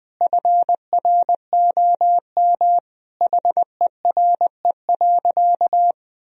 Morse code, heard as a keyed tone, about 20 wpm, 700 Hz. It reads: FROM HERE.
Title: Sentences from Top 1000 Words - Review Part 3 - 20wpm